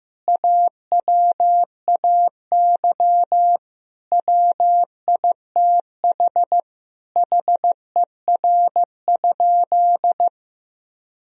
Morse code, heard as a keyed tone, about 15 wpm, 700 Hz. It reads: AWAY WITH HER?
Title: Sentences from Top 1000 Words - Review Part 3 - 15wpm